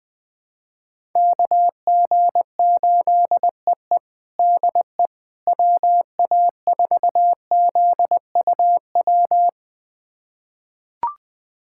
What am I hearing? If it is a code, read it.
Morse code, heard as a keyed tone, about 20 wpm, 700 Hz. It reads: KG8EE DE WA4ZUW
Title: USA Callsign DE Callsign 20wpm